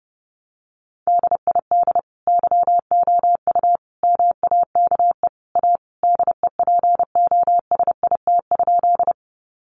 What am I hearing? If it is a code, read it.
Morse code, heard as a keyed tone, about 30 wpm, 700 Hz. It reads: DID YOU MAKE A DEPOSIT?
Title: Sentences from Top 1300 Words 30wpm